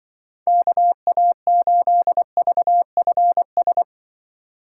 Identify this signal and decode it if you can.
Morse code, heard as a keyed tone, about 24 wpm, 700 Hz. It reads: KA8VFS
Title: Speed Racing - USA 2x3 Call Signs 20wpm